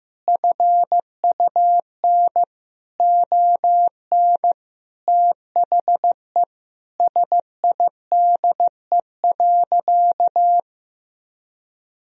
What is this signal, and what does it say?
Morse code, heard as a keyed tone, about 14 wpm, 700 Hz. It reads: FUN ON THE SIDE.
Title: Sentences from Top 1000 Words - Review Part 4 - 15wpm